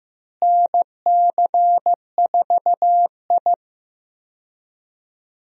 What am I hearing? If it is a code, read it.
Morse code, heard as a keyed tone, about 16 wpm, 700 Hz. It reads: NC4I
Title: USA Call Signs - 4 Characters - Repeated 2x - 15wpm